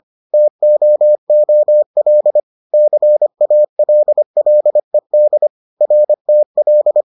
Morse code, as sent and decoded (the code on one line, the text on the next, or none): - --- --- .-.. / -.-. .- .-.. .-.. -.. / .-. - .-..
TOOL CALLED RTL